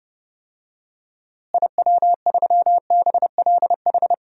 Morse code, heard as a keyed tone, about 30 wpm, 700 Hz. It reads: IW3BLH
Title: DXCC-100 Call Signs - 6 Characters - Repeated 3x - 30wpm